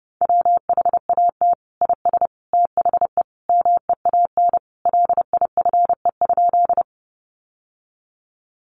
Morse code, heard as a keyed tone, about 30 wpm, 700 Hz. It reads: WHAT IS THE MEAN LIFE?
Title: Sentences from Top 900 Words 30wpm